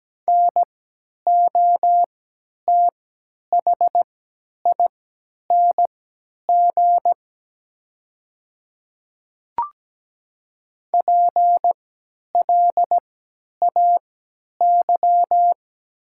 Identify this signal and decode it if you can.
Morse code, heard as a keyed tone, about 17 wpm, spaced as for 10 wpm, 700 Hz. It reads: NOTHING PLAY
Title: 3x Character Spacing - Top 500 Words - 17wpm